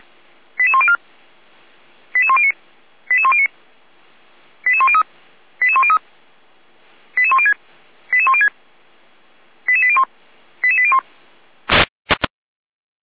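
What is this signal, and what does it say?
Five-tone selective call (ZVEI), heard as quick runs of five tones, tones of about 70 ms. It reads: 88104 88109 88109 88103 88103 88106 88106 88801 88801
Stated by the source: ZVEI five-tone call 88127 received by SDR.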